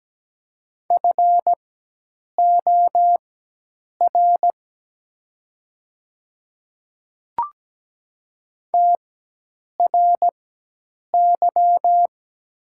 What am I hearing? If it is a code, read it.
Morse code, heard as a keyed tone, about 17 wpm, spaced as for 8 wpm, 700 Hz. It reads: FOR TRY
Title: 4x Character Spacing - Top 300 Words - 17wpm